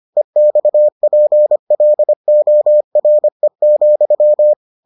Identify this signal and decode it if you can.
Morse code, heard as a keyed tone, about 25 wpm, 600 Hz. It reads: EXPLORE,